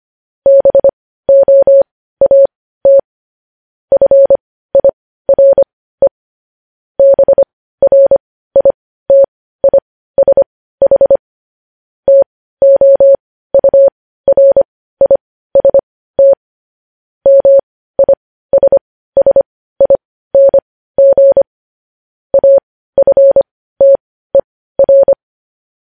Morse code, heard as a keyed tone, about 25 wpm, spaced as for 15 wpm, 550 Hz. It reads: BOAT FIRE BRITISH TOURIST MISSING AFTER
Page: news headlines in morse code at 15 wpm